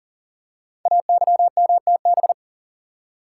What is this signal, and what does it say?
Morse code, heard as a keyed tone, about 38 wpm, 700 Hz. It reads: AYMTB
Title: Code Groups - 5 Letters - 40wpm